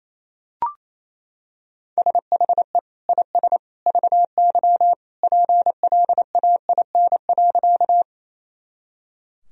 Morse code, heard as a keyed tone, about 27 wpm, 700 Hz. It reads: SHE IS VY PLAIN.
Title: Sentences from Top 400 Words and Abbreviations 28wpm